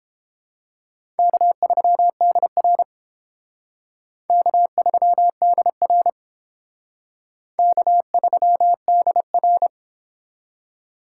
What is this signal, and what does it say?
Morse code, heard as a keyed tone, about 30 wpm, 700 Hz. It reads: K3DR K3DR K3DR